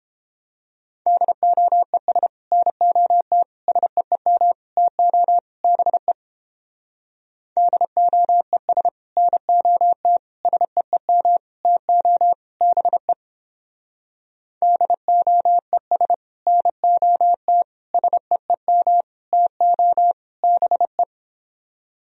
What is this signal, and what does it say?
Morse code, heard as a keyed tone, about 31 wpm, 700 Hz. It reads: DOES NOT SEEM TO BE DOES NOT SEEM TO BE DOES NOT SEEM TO BE